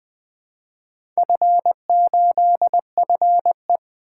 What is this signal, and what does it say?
Morse code, heard as a keyed tone, about 20 wpm, 700 Hz. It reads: F8FE
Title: Four Letter-Number Combinations - 20wpm